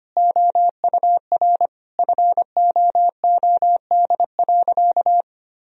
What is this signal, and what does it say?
Morse code, heard as a keyed tone, about 25 wpm, 700 Hz. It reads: OUR FOOD.
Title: Sentences from Top 1000 Words - Review Part 3 - 25wpm